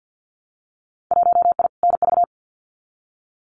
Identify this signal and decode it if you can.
Morse code, heard as a keyed tone, about 50 wpm, 700 Hz. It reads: 1I NV
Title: Winter Field Day Exchanges Encore 50wpm